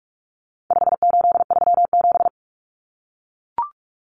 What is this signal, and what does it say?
Morse code, heard as a keyed tone, about 44 wpm, 700 Hz. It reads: 5837
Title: Numbers - 4-Digit 45wpm